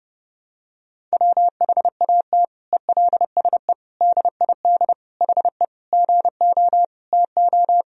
Morse code, heard as a keyed tone, about 32 wpm, 700 Hz. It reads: WHAT ELSE DID HE GO TO
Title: Sentences from Top 1000 Words - Review Part 1 - 30wpm